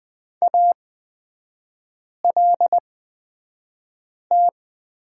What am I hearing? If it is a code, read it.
Morse code, heard as a keyed tone, about 20 wpm, spaced as for 5 wpm, 700 Hz. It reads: ALT
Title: 243 Lesson 39e - Letter z - Three Characters 20wpm